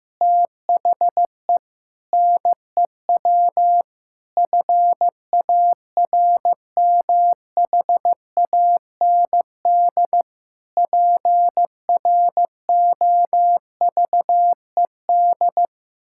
Morse code, heard as a keyed tone, about 15 wpm, 700 Hz. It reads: THE NEW FARMHAND PROVED
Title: Pangram Challenge v2 - 15wpm